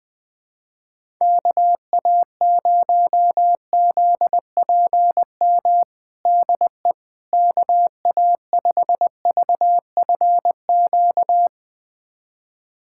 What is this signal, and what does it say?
Morse code, heard as a keyed tone, about 21 wpm, 700 Hz. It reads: KA0ZPM DE KA5VFQ
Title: USA Callsign DE Callsign - 6 Characters - 20wpm